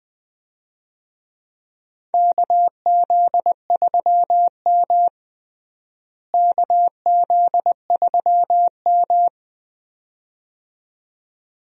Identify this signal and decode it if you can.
Morse code, heard as a keyed tone, about 20 wpm, 700 Hz. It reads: KZ3M KZ3M
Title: USA Call Signs - 4 Characters - Repeated 2x - 20wpm